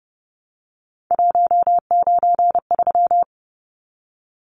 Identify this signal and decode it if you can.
Morse code, heard as a keyed tone, about 30 wpm, 700 Hz. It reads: 193
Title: Numbers - 3-Digit 30wpm